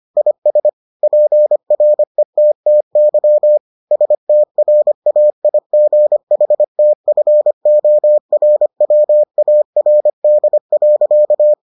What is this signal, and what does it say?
Morse code, heard as a keyed tone, about 25 wpm, 600 Hz. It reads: IS PRETTY STRAIGHTFORWARD.